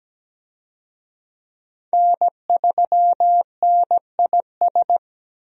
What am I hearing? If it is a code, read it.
Morse code, heard as a keyed tone, about 17 wpm, 700 Hz. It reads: N3NIS
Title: USA Call Signs - 5 Characters - Repeated 3x - 17wpm